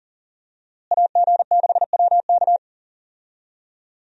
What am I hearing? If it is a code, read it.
Morse code, heard as a keyed tone, about 40 wpm, 700 Hz. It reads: AG6WK
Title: USA Call Signs - 5 Characters - Repeated 3x - 40wpm